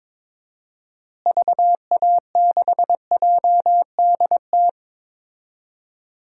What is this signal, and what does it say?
Morse code, heard as a keyed tone, about 21 wpm, 700 Hz. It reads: VA6JDT
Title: DXCC-100 Call Signs - 6 Characters - Repeated 3x - 22wpm